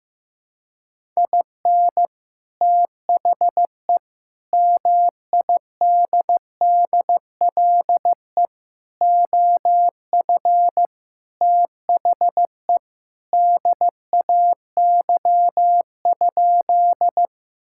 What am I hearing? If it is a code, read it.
Morse code, heard as a keyed tone, about 15 wpm, 700 Hz. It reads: IN THE MIDDLE OF THE DAY?
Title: Sentences from Top 1000 Words - Review Part 6 - 15wpm